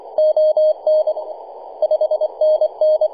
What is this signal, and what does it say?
Morse code, heard as a keyed tone, about 25 wpm, 620 Hz, with radio noise. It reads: ON 5NN
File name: e51dwc-21-010mhz-cw